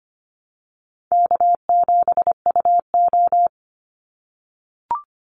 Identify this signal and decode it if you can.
Morse code, heard as a keyed tone, about 25 wpm, 700 Hz. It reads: K7UO
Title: USA Call Signs 25wpm